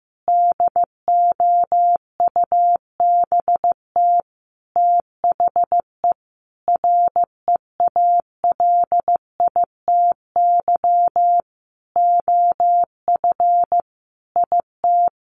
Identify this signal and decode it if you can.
Morse code, heard as a keyed tone, about 15 wpm, 700 Hz. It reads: DOUBT THE REALITY OF IT
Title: Sentences from Top 1500 Words 15wpm